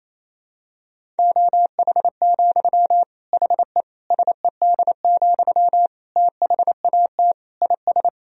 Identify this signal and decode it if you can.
Morse code, heard as a keyed tone, about 28 wpm, 700 Hz. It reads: OH, HE SED, THAT IS
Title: Sentences from Top 500 Words and Abbreviations 28wpm